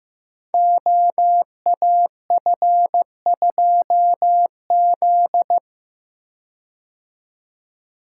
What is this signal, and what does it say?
Morse code, heard as a keyed tone, about 15 wpm, 700 Hz. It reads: OAF2Z
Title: Code Groups - 5 Letter:Numbers - 15wpm